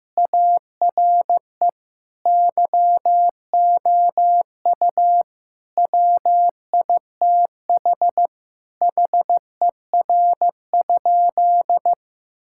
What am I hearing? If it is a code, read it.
Morse code, heard as a keyed tone, about 15 wpm, 700 Hz. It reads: ARE YOU WITH HER?
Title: Sentences from Top 1000 Words - Review Part 5 - 15wpm